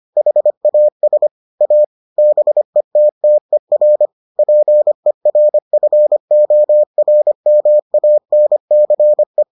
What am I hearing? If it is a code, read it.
Morse code, heard as a keyed tone, about 25 wpm, 600 Hz. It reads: HAS A BETTER PERFORMANCE